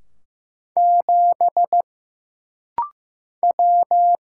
Morse code, heard as a keyed tone, about 15 wpm, 700 Hz. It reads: 7 W
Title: Single Letter-Number Alt - Rapid-Fire 15wpm